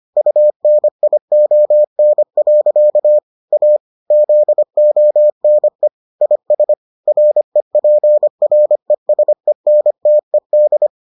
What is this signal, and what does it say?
Morse code, heard as a keyed tone, about 25 wpm, 600 Hz. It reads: UNION. A ZONE IS REPRESENTED